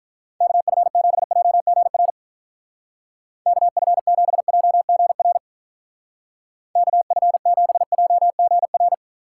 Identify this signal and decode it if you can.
Morse code, heard as a keyed tone, about 44 wpm, 700 Hz. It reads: KF7JGR KF7JGR KF7JGR